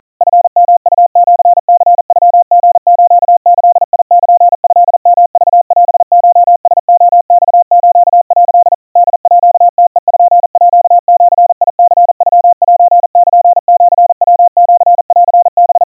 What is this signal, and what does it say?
Morse code (Japanese Wabun code), heard as a keyed tone, about 40 wpm, 700 Hz. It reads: チヨウシニノリスギルトヨクオコラレマス」 ホームページニノセルシヤシンハ
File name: kawa / Morse sample / kamyu 150 cpm 700 Hz Crispy